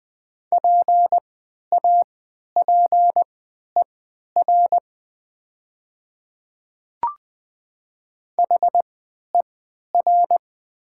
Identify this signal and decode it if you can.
Morse code, heard as a keyed tone, about 20 wpm, spaced as for 11 wpm, 700 Hz. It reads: PAPER HER
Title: 3x Character Spacing - Top 500 Words - 20wpm